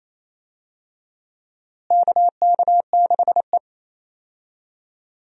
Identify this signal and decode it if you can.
Morse code, heard as a keyed tone, about 28 wpm, 700 Hz. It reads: KK6E
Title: USA Call Signs - 4 Characters - Repeated 3x - 28wpm